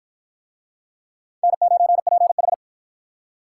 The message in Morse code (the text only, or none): N9PH